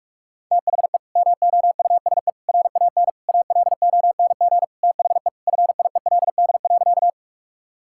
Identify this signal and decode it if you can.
Morse code, heard as a keyed tone, about 46 wpm, 700 Hz. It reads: THE MOUSE RAN ALONG THE FIELD.
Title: Sentences from Top 1000 Words - Review Part 5 - 45wpm